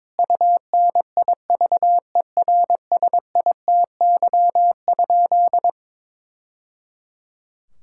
Morse code, heard as a keyed tone, about 22 wpm, 700 Hz. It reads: UNIVERSITY?
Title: Sentences from Top 1000 Words - Review Part 6 - 22wpm